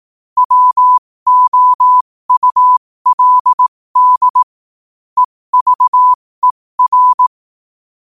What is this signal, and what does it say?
Morse code, heard as a keyed tone, about 18 wpm, spaced as for 15 wpm, 1000 Hz. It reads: WOULD EVER
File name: qotdfiles / 15wpm